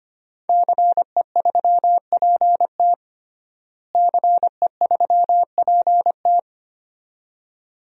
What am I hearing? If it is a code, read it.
Morse code, heard as a keyed tone, about 25 wpm, 700 Hz. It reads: CE3PT CE3PT
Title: DXCC-100 Call Signs - 5 Characters - Repeated 2x - 25wpm